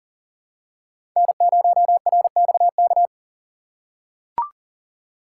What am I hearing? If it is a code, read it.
Morse code, heard as a keyed tone, about 40 wpm, 700 Hz. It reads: N0RXK
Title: USA Call Signs - 6x Character Spacing - 40wpm